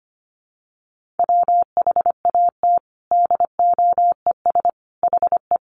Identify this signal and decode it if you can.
Morse code, heard as a keyed tone, about 25 wpm, 700 Hz. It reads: WHAT DOES HE
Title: Sentences from Top 200 Words 25wpm